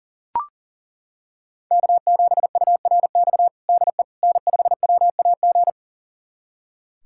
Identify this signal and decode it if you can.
Morse code, heard as a keyed tone, about 40 wpm, 700 Hz. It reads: K7URX DE N5WAG
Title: USA Callsign DE Callsign - 5 Characters - 40wpm